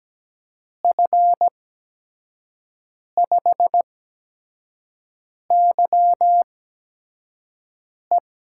Morse code, heard as a keyed tone, about 17 wpm, spaced as for 5 wpm, 700 Hz. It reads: F5YE
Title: USA Call Signs - 6 Characters - 8x Character Spacing - 17wpm